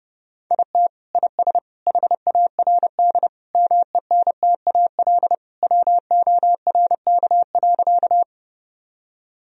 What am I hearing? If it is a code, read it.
Morse code, heard as a keyed tone, about 32 wpm, 700 Hz. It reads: IT IS HARD MENTAL WORK.
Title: Sentences from Top 1000 Words - Review Part 4 - 30wpm